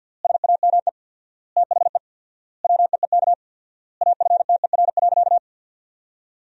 Morse code, heard as a keyed tone, about 50 wpm, 700 Hz. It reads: SAME THE WEEK AFTER.
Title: 4x Word-Spacing - Sentences from Top 500 Words - 50wpm